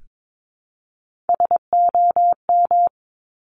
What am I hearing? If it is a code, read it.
Morse code, heard as a keyed tone, about 22 wpm, 700 Hz. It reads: SOM
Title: Top Three Letter Combinations in Words 22wpm